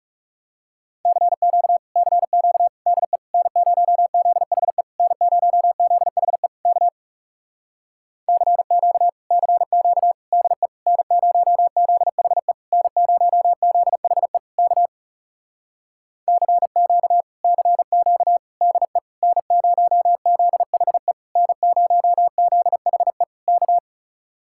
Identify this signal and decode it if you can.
Morse code, heard as a keyed tone, about 40 wpm, 700 Hz. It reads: CQ CQ DE N0ZHE N0ZHE K CQ CQ DE N0ZHE N0ZHE K CQ CQ DE N0ZHE N0ZHE K